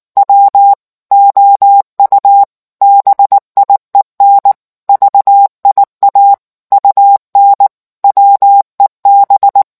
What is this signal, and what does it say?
Morse code, heard as a keyed tone, about 19 wpm, 800 Hz. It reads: W OU BIEN VIA UN WEB